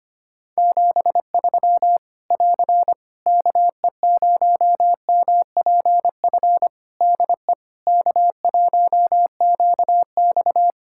Morse code, heard as a keyed tone, about 25 wpm, 700 Hz. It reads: 73 <AR> KE0MPF DE K1QX
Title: Practice QSO - Exchange 4 - Ending - 25wpm